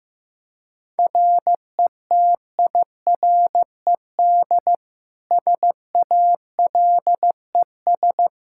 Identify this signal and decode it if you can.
Morse code, heard as a keyed tone, about 15 wpm, 700 Hz. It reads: RETIRED SALES
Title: Practice QSO - Exchange 3 - More Info - 15wpm